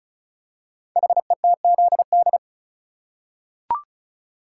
Morse code, heard as a keyed tone, about 36 wpm, 690 Hz. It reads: HETZD
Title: Code Groups - 5 Letters - 35wpm